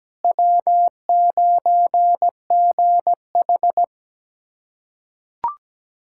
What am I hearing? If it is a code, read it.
Morse code, heard as a keyed tone, about 17 wpm, 700 Hz. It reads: W9GH
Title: USA Callsign DE Callsign - 4 Characters - 17wpm